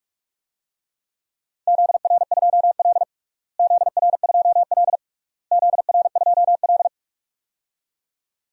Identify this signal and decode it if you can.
Morse code, heard as a keyed tone, about 46 wpm, 700 Hz. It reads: ZR2L ZR2L ZR2L
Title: DXCC-100 Call Signs - 4 Characters - Repeated 3x - 45wpm